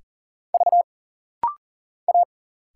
Morse code, heard as a keyed tone, about 40 wpm, 700 Hz. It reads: V A
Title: Single Letters - Rapid-Fire 40wpm